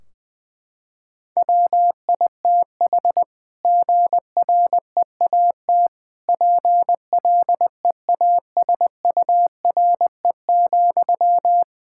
Morse code, heard as a keyed tone, about 21 wpm, 700 Hz. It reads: WITH GREAT PLEASURE,
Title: Sentences from Top 1600 Words 20wpm